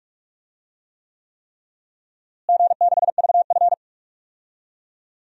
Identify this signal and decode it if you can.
Morse code, heard as a keyed tone, about 43 wpm, 700 Hz. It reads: G6VF
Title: DXCC-100 Call Signs - 4 Characters - Repeated 3x - 45wpm